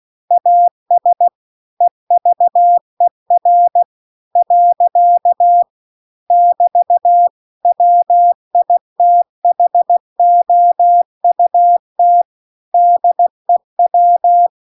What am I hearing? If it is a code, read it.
Morse code, heard as a keyed tone, about 16 wpm, 700 Hz. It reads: AS EVER . <BT> WITHOUT DEW
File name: CW-The-Big-Twoheaded-River-2-16WPM